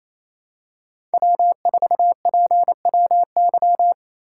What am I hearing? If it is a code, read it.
Morse code, heard as a keyed tone, about 28 wpm, 700 Hz. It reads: W4PWY
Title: USA Call Signs - 8x Character Spacing - 28wpm